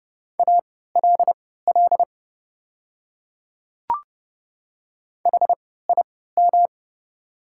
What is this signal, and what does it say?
Morse code, heard as a keyed tone, about 30 wpm, spaced as for 17 wpm, 700 Hz. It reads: ALL HIM
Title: 3x Character Spacing - Top 100 Words - 30wpm